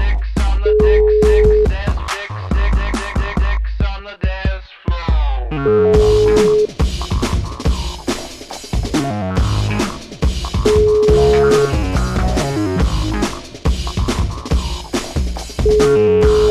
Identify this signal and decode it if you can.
Ringback tone (Europe, 425 Hz), with noise.